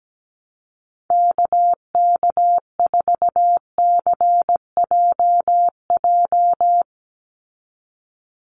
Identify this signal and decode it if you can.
Morse code, heard as a keyed tone, about 17 wpm, 700 Hz. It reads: KK4CJJ
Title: USA Call Signs 17wpm